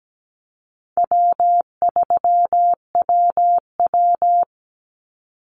Morse code, heard as a keyed tone, about 17 wpm, 700 Hz. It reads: W3WW
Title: Easy Call Signs Repeated 2x - 17wpm